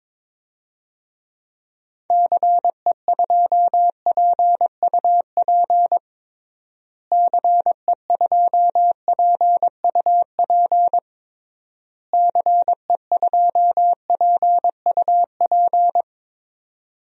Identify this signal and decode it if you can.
Morse code, heard as a keyed tone, about 22 wpm, 700 Hz. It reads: CE2PUP CE2PUP CE2PUP
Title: DXCC-100 Call Signs - 6 Characters - Repeated 3x - 22wpm